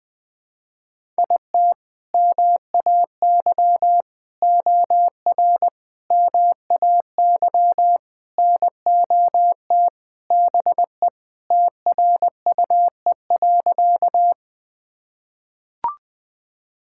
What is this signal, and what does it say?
Morse code, heard as a keyed tone, about 20 wpm, 700 Hz. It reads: IT MAY OR MAY NOT BE TRUE.
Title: Sentences with 5-Word Phrases - 20wpm